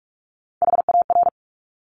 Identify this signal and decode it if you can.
Morse code, heard as a keyed tone, about 46 wpm, 690 Hz. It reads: HAR
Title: Top Three Letter Combinations in Words 45wpm